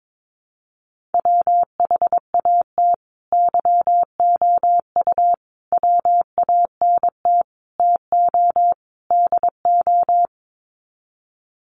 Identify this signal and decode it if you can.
Morse code, heard as a keyed tone, about 22 wpm, 700 Hz. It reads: WHAT YOU WANT TO DO
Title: Sets of 5 Words Encore 22wpm